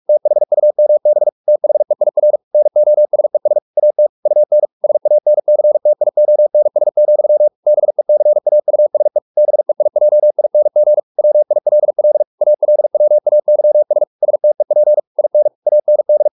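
Morse code, heard as a keyed tone, about 44 wpm, 600 Hz. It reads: THUMB THEIR NOSES AT UN SANCTIONS, BECAUSE BEIJING WILL ALWAYS STEP IN AND
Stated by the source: Morse practice at 45 wpm, the novel Never by Ken Follett in English, from CW med Gnister.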